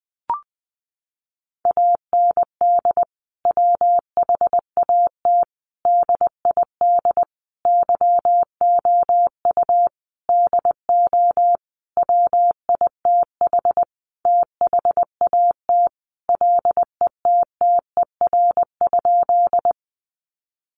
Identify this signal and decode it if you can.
Morse code, heard as a keyed tone, about 20 wpm, 700 Hz. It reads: AND WHAT DID YOU DO WITH THAT LETTER?
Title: Sentences from Top 300 Words 20wpm